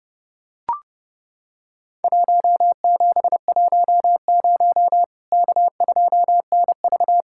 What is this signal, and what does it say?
Morse code, heard as a keyed tone, about 30 wpm, 700 Hz. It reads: 1710 K2NV
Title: Worked All Europe DX Contest - QTC Exchange - 30wpm